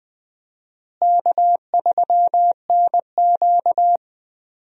Morse code, heard as a keyed tone, about 20 wpm, 700 Hz. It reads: K3NQ
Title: USA Call Signs - 4 Characters - Repeated 2x - 20wpm